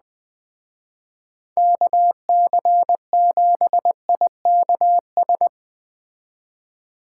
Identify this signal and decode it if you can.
Morse code, heard as a keyed tone, about 20 wpm, 700 Hz. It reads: KC7IKS
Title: Speed Racing - USA 2x3 Call Signs 17wpm